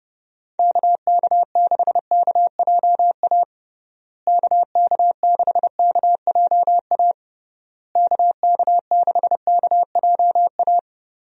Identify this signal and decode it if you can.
Morse code, heard as a keyed tone, about 30 wpm, 700 Hz. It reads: KK6KJA KK6KJA KK6KJA
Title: USA Call Signs - 6 Characters - Repeated 3x - 30wpm